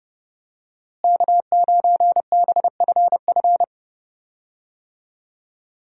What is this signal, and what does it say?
Morse code, heard as a keyed tone, about 32 wpm, 700 Hz. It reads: K9BFF
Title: USA Call Signs - 5 Characters - Repeated 2x - 30wpm